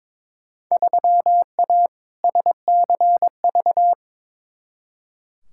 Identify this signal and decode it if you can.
Morse code, heard as a keyed tone, about 22 wpm, 700 Hz. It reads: 3A SCV
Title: Field Day Exchanges - 22wpm